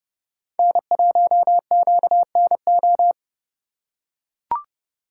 Morse code, heard as a keyed tone, about 31 wpm, 700 Hz. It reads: N1QNO
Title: USA Call Signs - 4x Character Spacing - 30wpm